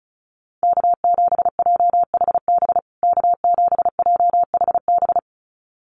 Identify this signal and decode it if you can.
Morse code, heard as a keyed tone, about 34 wpm, 700 Hz. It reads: K7JHB K7JHB
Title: USA Call Signs Repeated 2x 35wpm